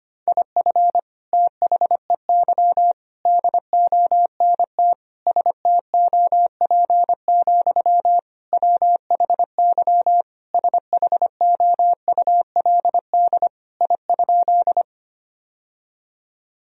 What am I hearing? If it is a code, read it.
Morse code, heard as a keyed tone, about 25 wpm, 700 Hz. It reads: IF THEY DONT STOP, WHY SHOULD I?
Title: Sentences from Top 1000 Words - Review Part 1 - 25wpm